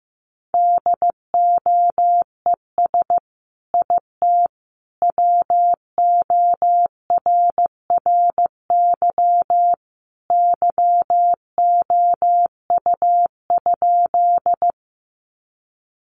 Morse code, heard as a keyed tone, about 15 wpm, 700 Hz. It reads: DOES IT WORRY YOU?
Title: Sentences from Top 1100 Words 15wpm